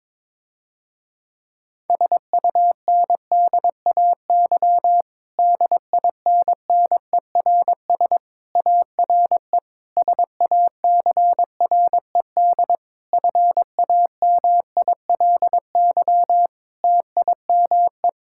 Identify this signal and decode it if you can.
Morse code, heard as a keyed tone, about 22 wpm, 700 Hz. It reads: SUNDAY DINNERS ARE SACRED FAMILY TIME
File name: Difficult Sentences with Hints - 22wpm